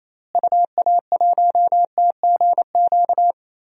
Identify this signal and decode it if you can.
Morse code, heard as a keyed tone, about 28 wpm, 700 Hz. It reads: UA1TGQ
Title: DXCC-100 Call Signs - 6 Characters - Repeated 2x - 28wpm